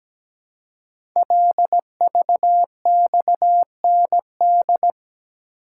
Morse code, heard as a keyed tone, about 17 wpm, 700 Hz. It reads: LVXND
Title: Code Groups - 5 Letters - 17wpm